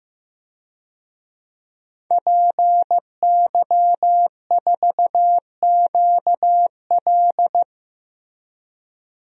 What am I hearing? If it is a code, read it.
Morse code, heard as a keyed tone, about 15 wpm, 700 Hz. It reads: PY4QL
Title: DXCC-100 Call Signs - 5 Characters - Repeated 2x - 15wpm